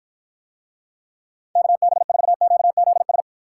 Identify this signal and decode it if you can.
Morse code, heard as a keyed tone, about 54 wpm, 700 Hz. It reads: KB4QZS